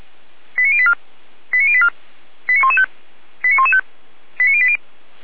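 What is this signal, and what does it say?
Five-tone selective call (ZVEI), heard as quick runs of five tones, tones of about 80 ms. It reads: 89084 89084 89115 89115 89080
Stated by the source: Five-tone ZVEI selective call 89101, received by SDR.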